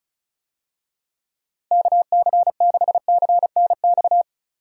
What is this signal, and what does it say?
Morse code, heard as a keyed tone, about 34 wpm, 700 Hz. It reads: KC6CNX
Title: USA Call Signs - 6 Characters - 35wpm